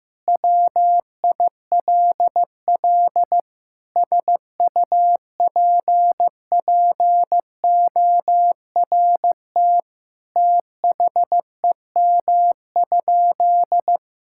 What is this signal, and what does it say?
Morse code, heard as a keyed tone, about 16 wpm, 700 Hz. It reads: WILL SUPPORT THEM?
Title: Sentences from Top 1000 Words - Review Part 5 - 15wpm